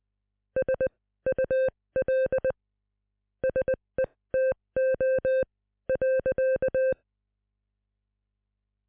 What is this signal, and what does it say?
Morse code, heard as a keyed tone, about 20 wpm, spaced as for 16 wpm, 520 Hz. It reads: SUL SETO .